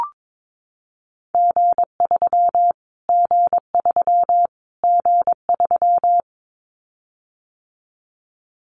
Morse code, heard as a keyed tone, about 22 wpm, 700 Hz. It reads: G3 G3 G3
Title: Callsign Prefixes for Top 25 DXCC - Repeated 3x - 22wpm